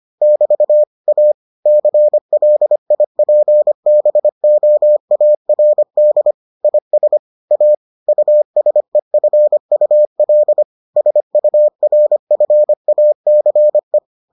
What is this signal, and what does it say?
Morse code, heard as a keyed tone, about 25 wpm, 600 Hz. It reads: <BT> A CLIPBOARD IS A USEFUL SURFACE